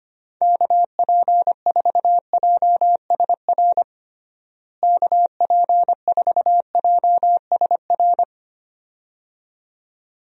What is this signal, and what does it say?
Morse code, heard as a keyed tone, about 25 wpm, 700 Hz. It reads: KP4JSR KP4JSR
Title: USA Call Signs - 6 Characters - Repeated 2x - 25wpm